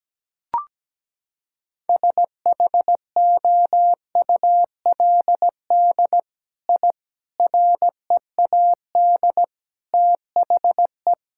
Morse code, heard as a keyed tone, about 17 wpm, 700 Hz. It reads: SHOULD I READ THE
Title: Sentences from Top 1000 Words - Review Part 5 - 17wpm